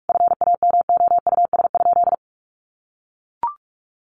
Morse code, heard as a keyed tone, about 44 wpm, 700 Hz. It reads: FAMOUS?
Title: Sentences from Top 1500 Words 45wpm